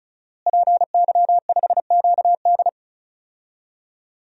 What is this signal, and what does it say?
Morse code, heard as a keyed tone, about 36 wpm, 700 Hz. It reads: PY5QD
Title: DXCC-100 Call Signs - 5 Characters - Repeated 3x - 35wpm